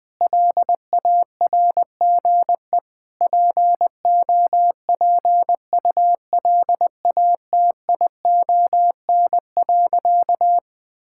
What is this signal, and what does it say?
Morse code, heard as a keyed tone, about 21 wpm, 690 Hz. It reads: LARGE POPULATION.
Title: Sentences with 4-Word Phrases - Spoken Beginning - 20wpm